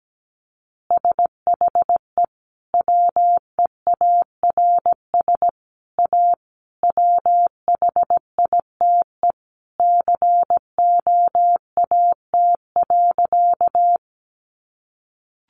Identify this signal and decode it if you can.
Morse code, heard as a keyed tone, about 17 wpm, 700 Hz. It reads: SHE WEARS A WHITE COAT.
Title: Additional Easy Phrases 17wpm